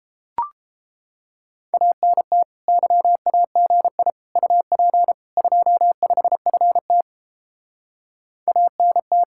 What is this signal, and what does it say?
Morse code, heard as a keyed tone, about 29 wpm, 700 Hz. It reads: ANT YAGI UP 25FT ANT